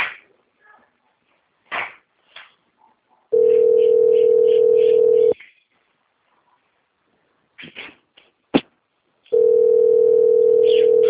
Ringback tone (North America), 440 + 480 Hz.